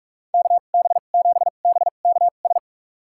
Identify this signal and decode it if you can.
Morse code, heard as a keyed tone, about 42 wpm, spaced as for 31 wpm, 700 Hz. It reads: KB7BKS
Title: USA Call Signs - 2x Character Spacing - 45wpm